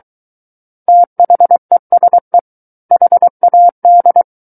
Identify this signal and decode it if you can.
Morse code, heard as a keyed tone, about 23 wpm, 700 Hz. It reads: THESE HAD
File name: Book One / 14-Chapter 14 - 23 WPM